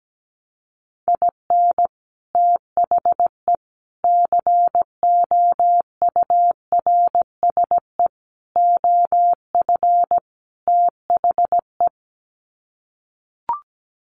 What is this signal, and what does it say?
Morse code, heard as a keyed tone, about 17 wpm, 700 Hz. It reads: IN THE COURSE OF THE
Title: Sets of 5 Words 17wpm